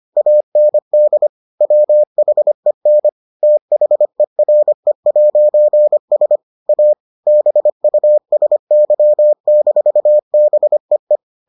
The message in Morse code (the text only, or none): AND WHEN THERE'S A BUSY-BEE